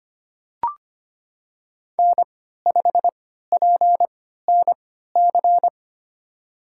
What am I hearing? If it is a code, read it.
Morse code, heard as a keyed tone, about 25 wpm, spaced as for 14 wpm, 700 Hz. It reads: N5PNC
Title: USA Call Signs - 3x Character Spacing - 25wpm